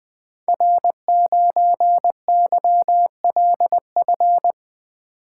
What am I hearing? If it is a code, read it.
Morse code, heard as a keyed tone, about 20 wpm, 700 Hz. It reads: R9YLF
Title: Code Groups - 5 Letter:Numbers - 20wpm